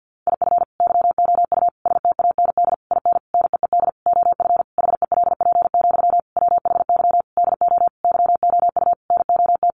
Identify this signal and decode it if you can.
Morse code, heard as a keyed tone, about 50 wpm, 700 Hz. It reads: IF YOU STAND IN NEED OF HELP, WHY DO YOU NOT
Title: Sentences from Top 1200 Words 50wpm